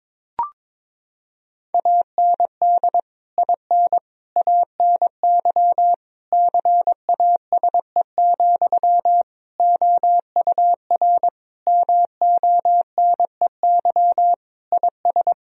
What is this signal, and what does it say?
Morse code, heard as a keyed tone, about 22 wpm, 700 Hz. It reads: AND IN ANY CASE, OUR MONEY IS
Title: Sentences from Top 1000 Words - Review Part 1 - 22wpm